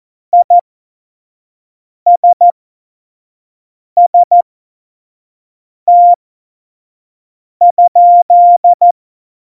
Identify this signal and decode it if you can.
Morse code, heard as a keyed tone, about 14 wpm, spaced as for 5 wpm, 700 Hz. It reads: ISST?
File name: Morsekurs / Witze / Witz 26 Mai 2020, 25 F70